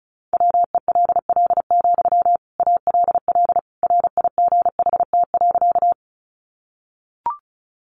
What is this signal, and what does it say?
Morse code, heard as a keyed tone, about 34 wpm, 700 Hz. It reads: WELL, ALL RIGHT.
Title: Sentences from Top 200 Words 35wpm